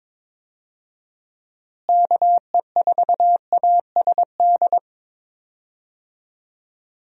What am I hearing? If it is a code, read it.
Morse code, heard as a keyed tone, about 22 wpm, 700 Hz. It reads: KE4ASD
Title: USA Call Signs - 6 Characters - Repeated 2x - 22wpm